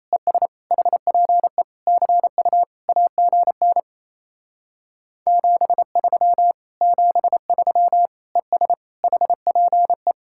Morse code, heard as a keyed tone, about 38 wpm, 700 Hz. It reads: ES HPE CU AGN 73 73 ES HPE